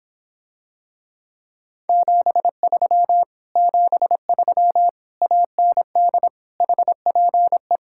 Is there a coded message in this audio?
Yes: Morse code from 1.9 to 7.8 s, heard as a keyed tone, about 26 wpm, 700 Hz. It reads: 73 73 AND HPE